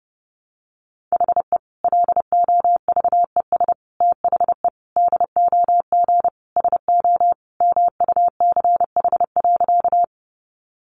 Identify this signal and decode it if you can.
Morse code, heard as a keyed tone, about 30 wpm, 700 Hz. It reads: HE LOVES THE DOG SO MUCH.
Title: Additional Easy Phrases 30wpm